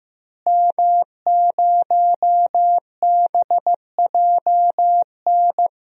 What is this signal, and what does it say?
Morse code, heard as a keyed tone, about 15 wpm, 700 Hz. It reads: M0BJN
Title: DXCC-100 Call Signs - 5 Characters - Repeated 3x - 15wpm